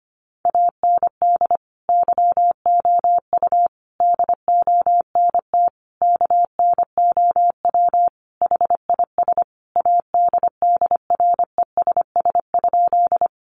Morse code, heard as a keyed tone, about 25 wpm, 700 Hz. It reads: AND YOU DONT KNOW HIS ADDRESS?
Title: Sentences from Top 1000 Words 25wpm